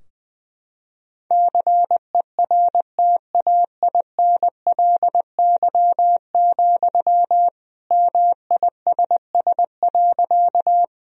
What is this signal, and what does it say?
Morse code, heard as a keyed tone, about 20 wpm, 700 Hz. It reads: CERTAINLY, MISS.
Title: Sentences from Top 1000 Words - Review Part 3 - 20wpm